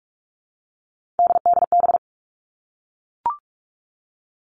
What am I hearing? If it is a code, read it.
Morse code, heard as a keyed tone, about 44 wpm, 700 Hz. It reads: DDB
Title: All Cut Numbers - 3-Digit 45wpm